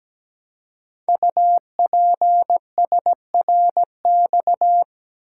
Morse code, heard as a keyed tone, about 17 wpm, 700 Hz. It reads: UPSRX
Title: Code Groups - 5 Letters - 17wpm